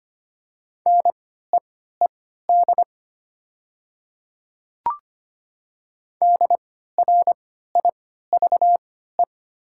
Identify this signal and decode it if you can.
Morse code, heard as a keyed tone, about 25 wpm, spaced as for 14 wpm, 700 Hz. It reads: NEED DRIVE
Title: 3x Character Spacing - Top 500 Words - 25wpm